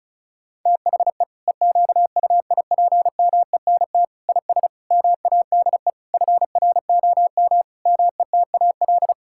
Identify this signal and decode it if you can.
Morse code, heard as a keyed tone, about 34 wpm, 700 Hz. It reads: THE EQUIPMENT IS MADE FROM METAL
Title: Sentences from Top 1000 Words - Review Part 6 - 35wpm